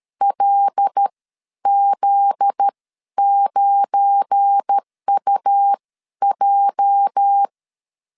Morse code, heard as a keyed tone, about 13 wpm, spaced as for 10 wpm, 800 Hz. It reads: LZ9UJ
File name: LEZIONI / 8 YJ45 / 8 YJ 12wpm 2235